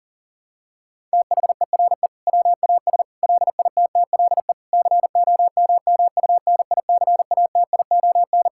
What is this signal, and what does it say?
Morse code, heard as a keyed tone, about 40 wpm, 700 Hz. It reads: THERE WAS LITTLE COMMUNICATION
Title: Sentences from Top 1000 Words - Review Part 6 - 40wpm